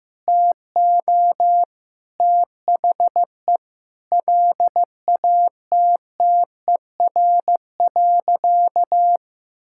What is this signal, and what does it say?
Morse code, heard as a keyed tone, about 15 wpm, 700 Hz. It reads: TO THE LATTER.
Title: Sentences from Top 1600 Words 15wpm